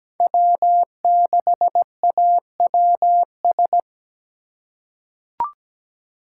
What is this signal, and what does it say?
Morse code, heard as a keyed tone, about 17 wpm, 700 Hz. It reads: W6AWS
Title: K1USN SST QSO Practice - 17wpm